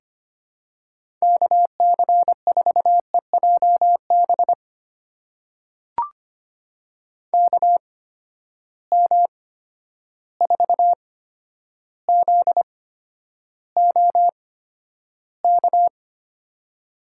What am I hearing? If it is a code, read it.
Morse code, heard as a keyed tone, about 25 wpm, spaced as for 11 wpm, 700 Hz. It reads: KC4EJB KM4ZOK
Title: USA Call Signs - 6 Characters - 8x Character Spacing - 25wpm